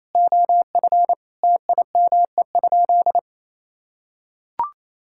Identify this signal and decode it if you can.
Morse code, heard as a keyed tone, about 28 wpm, 700 Hz. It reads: OF TIME?
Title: Sentences from Top 1000 Words - Review Part 5 - 28wpm